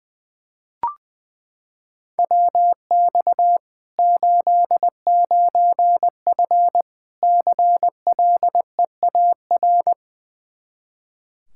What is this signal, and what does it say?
Morse code, heard as a keyed tone, about 20 wpm, 700 Hz. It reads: WX 89F CLEAR
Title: Practice QSO - Exchange 2 - Info - 20wpm